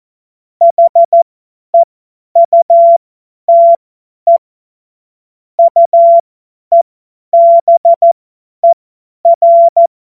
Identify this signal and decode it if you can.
Morse code, heard as a keyed tone, about 14 wpm, spaced as for 10 wpm, 680 Hz. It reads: HEUTE UEBER
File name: Morsekurs / Witze / Witz 02 Januar 2024 50 F70